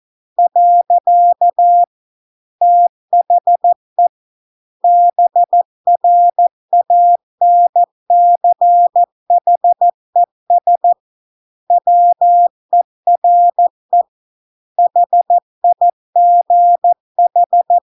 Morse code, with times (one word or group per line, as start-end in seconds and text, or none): 0.4-1.8 .
2.6-4.1 THE
4.8-10.9 BRANCHES
11.7-14.0 WERE
14.8-17.8 HIGH